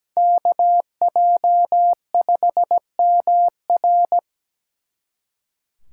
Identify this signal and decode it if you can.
Morse code, heard as a keyed tone, about 20 wpm, 700 Hz. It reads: KJ5MR
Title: Speed Racing - USA 2x2 Call Signs 17wpm